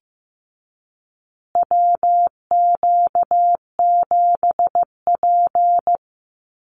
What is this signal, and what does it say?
Morse code, heard as a keyed tone, about 15 wpm, 700 Hz. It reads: WQ7P
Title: USA Call Signs Repeated 2x 15wpm